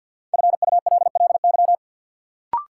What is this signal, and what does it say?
Morse code, heard as a keyed tone, about 50 wpm, 700 Hz. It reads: FULLY
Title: Sentences from Top 1300 Words - New Word Drill - 50wpm